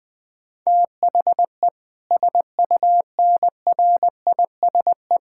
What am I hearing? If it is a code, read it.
Morse code, heard as a keyed tone, about 20 wpm, 700 Hz. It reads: THE SUNRISE
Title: Difficult Sentences with Hints - 20wpm